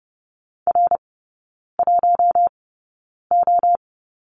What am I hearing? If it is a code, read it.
Morse code, heard as a keyed tone, about 30 wpm, spaced as for 9 wpm, 700 Hz. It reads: R1O
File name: 081 Lesson 14e - Number 2 - Three Characters 30wpm